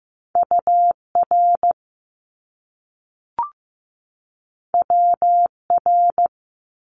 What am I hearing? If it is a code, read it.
Morse code, heard as a keyed tone, about 15 wpm, 700 Hz. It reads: UR WR
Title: Top Two Letter Combinations in Words 15wpm